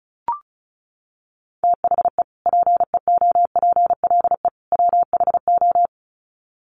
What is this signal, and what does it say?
Morse code, heard as a keyed tone, about 34 wpm, 700 Hz. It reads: THE PEOPLE WHO
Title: Sets of 3 Words 35wpm